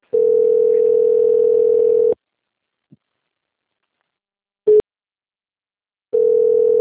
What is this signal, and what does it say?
Ringback tone (North America), 440 + 480 Hz.